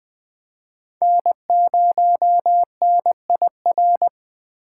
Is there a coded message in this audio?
Yes: Morse code from 1.0 to 4.1 s, heard as a keyed tone, about 23 wpm, 700 Hz. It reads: N0NIR